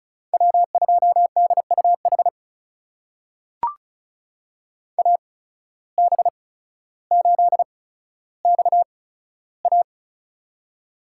Morse code, heard as a keyed tone, about 36 wpm, spaced as for 15 wpm, 700 Hz. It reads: W2DUH AB8XA
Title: USA Call Signs - 5 Characters - 8x Character Spacing - 35wpm